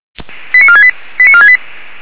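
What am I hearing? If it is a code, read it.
Five-tone selective call (ZVEI), heard as quick runs of five tones, tones of about 80 ms. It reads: 89468 89468
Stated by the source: ZVEI five-tone selective call 89145, received via SDR on channel 464.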